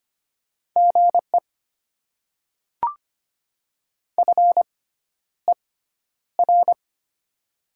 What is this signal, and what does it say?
Morse code, heard as a keyed tone, about 25 wpm, spaced as for 11 wpm, 700 Hz. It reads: GE FER
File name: Top 20 QSO Elements - 6x Character Spacing - 25wpm